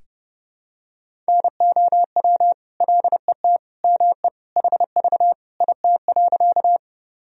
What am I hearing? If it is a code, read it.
Morse code, heard as a keyed tone, about 30 wpm, 700 Hz. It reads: NOW LET ME HV IT.
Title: Sentences from Top 300 Words and Abbreviations 30wpm